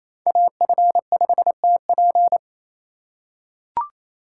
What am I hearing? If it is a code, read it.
Morse code, heard as a keyed tone, about 27 wpm, 700 Hz. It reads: AF5TP
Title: USA Call Signs - 5 Characters - 4x Character Spacing - 28wpm